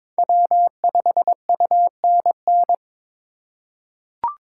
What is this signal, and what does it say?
Morse code, heard as a keyed tone, about 22 wpm, 700 Hz. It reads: W5UNN
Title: USA Call Signs - 8x Character Spacing - 22wpm